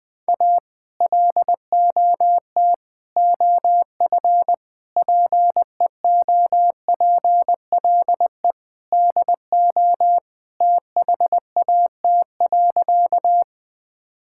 Morse code, heard as a keyed tone, about 21 wpm, 700 Hz. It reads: A LOT OF PEOPLE DO THAT.
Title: Sentences with 5-Word Phrases - 20wpm